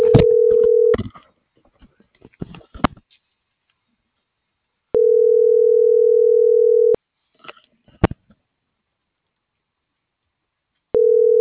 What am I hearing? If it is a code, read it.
Ringback tone (North America), 440 + 480 Hz.